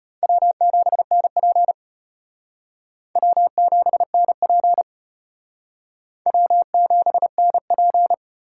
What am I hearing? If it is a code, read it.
Morse code, heard as a keyed tone, about 33 wpm, 700 Hz. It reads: W7NP W7NP W7NP